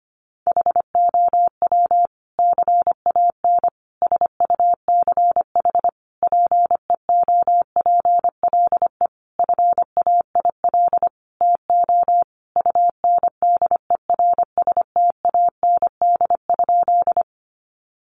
Morse code, heard as a keyed tone, about 25 wpm, 700 Hz. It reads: HOW CAN SUCH PEOPLE FAIL TO UNDERSTAND?
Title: Sentences from Top 1100 Words 25wpm